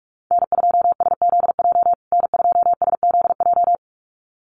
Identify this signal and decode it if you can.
Morse code, heard as a keyed tone, about 44 wpm, 700 Hz. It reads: N2SZJ N2SZJ
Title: USA Call Signs Repeated 2x 45wpm